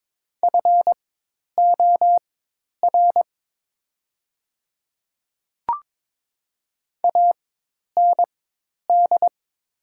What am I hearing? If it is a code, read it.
Morse code, heard as a keyed tone, about 22 wpm, spaced as for 10 wpm, 700 Hz. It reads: FOR AND
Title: 4x Character Spacing - Top 100 Words - 22wpm